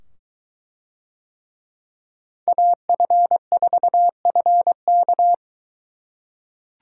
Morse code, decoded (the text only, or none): AF4FK